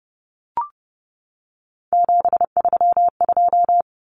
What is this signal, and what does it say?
Morse code, heard as a keyed tone, about 30 wpm, 700 Hz. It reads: 732
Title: Numbers - 3-Digit 30wpm